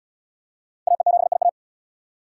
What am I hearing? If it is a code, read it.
Morse code, heard as a keyed tone, about 74 wpm, 690 Hz. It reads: AE7IA